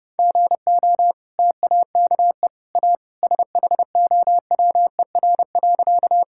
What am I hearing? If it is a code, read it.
Morse code, heard as a keyed tone, about 32 wpm, 700 Hz. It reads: GO TAKE A SHOWER.
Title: Sentences from Top 1600 Words 30wpm